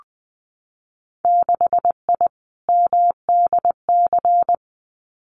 Morse code, heard as a keyed tone, about 20 wpm, 700 Hz. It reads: 6I MDC
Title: Winter Field Day Exchanges Encore 20wpm